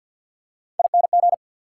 Morse code, heard as a keyed tone, about 47 wpm, 690 Hz. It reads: ING
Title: Prefixes and Suffixes 50wpm